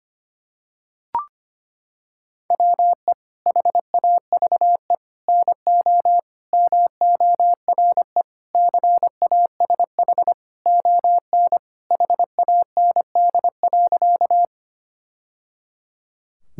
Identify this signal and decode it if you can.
Morse code, heard as a keyed tone, about 25 wpm, 700 Hz. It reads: WE HAVE NO MORE CASH ON HAND.
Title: Sentences from Top 1000 Words - Review Part 5 - 25wpm